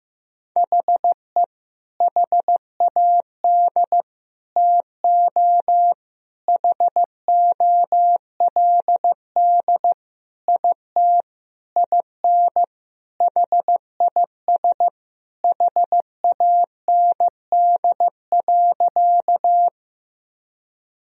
Morse code, heard as a keyed tone, about 16 wpm, 700 Hz. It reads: HE HAD TO HOLD IT IN HIS HAND.
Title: Sentences from Top 1000 Words - Review Part 2 - 15wpm